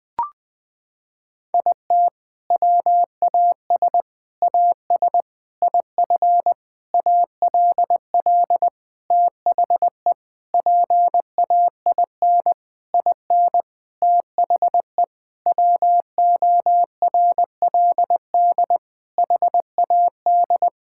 Morse code, heard as a keyed tone, about 20 wpm, 700 Hz. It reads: IT WAS AS IF ALL THE PAIN IN THE WORLD HAD
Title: Sentences from Top 1600 Words 20wpm